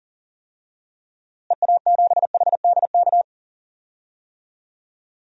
Morse code, heard as a keyed tone, about 40 wpm, 700 Hz. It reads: EA7HDK
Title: DXCC-100 Call Signs - 6 Characters - Repeated 3x - 40wpm